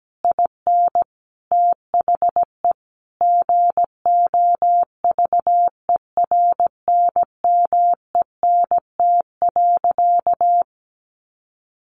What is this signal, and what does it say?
Morse code, heard as a keyed tone, about 17 wpm, 700 Hz. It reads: IN THE GOVERNMENT.
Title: Sentences from Top 1300 Words 17wpm